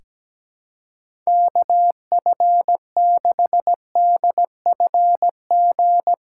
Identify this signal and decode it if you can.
Morse code, heard as a keyed tone, about 17 wpm, 700 Hz. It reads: KF6DFG
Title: USA Callsign DE Callsign - 6 Characters - 17wpm